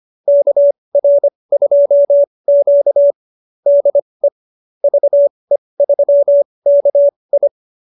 Morse code, heard as a keyed tone, about 25 wpm, spaced as for 20 wpm, 590 Hz. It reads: KR2Q DE VE3KI